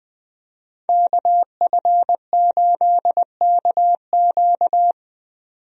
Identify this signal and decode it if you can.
Morse code, heard as a keyed tone, about 20 wpm, 700 Hz. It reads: KF8KQ
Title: USA Call Signs - 5 Characters - 20wpm